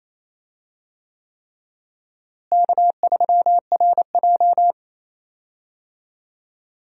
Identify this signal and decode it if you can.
Morse code, heard as a keyed tone, about 28 wpm, 700 Hz. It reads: K3RJ
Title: USA Call Signs - 4 Characters - Repeated 3x - 28wpm